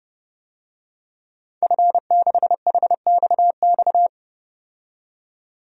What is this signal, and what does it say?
Morse code, heard as a keyed tone, about 30 wpm, 700 Hz. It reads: F6HXX
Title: DXCC-100 Call Signs - 5 Characters - 30wpm